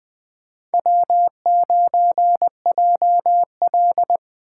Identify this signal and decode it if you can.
Morse code, heard as a keyed tone, about 23 wpm, 700 Hz. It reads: W9JL
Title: Speed Racing - USA 1x2 Call Signs 20wpm